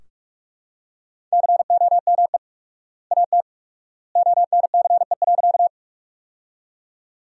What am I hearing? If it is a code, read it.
Morse code, heard as a keyed tone, about 41 wpm, 700 Hz. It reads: COME AT ONCE.
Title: 4x Word-Spacing - Sentences from Top 300 Words - 45wpm